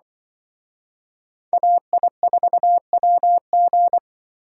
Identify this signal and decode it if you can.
Morse code, heard as a keyed tone, about 24 wpm, 700 Hz. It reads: AI4WG
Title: Speed Racing - USA Call Signs 20wpm